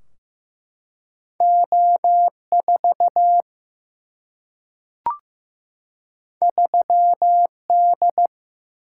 Morse code, heard as a keyed tone, about 15 wpm, 700 Hz. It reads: O4 3D
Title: Single-Letter Single-Number 15wpm